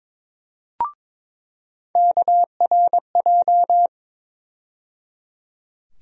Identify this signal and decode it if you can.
Morse code, heard as a keyed tone, about 22 wpm, 700 Hz. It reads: KRJ
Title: USA Call Sign Postfixes - 3 Letters 22wpm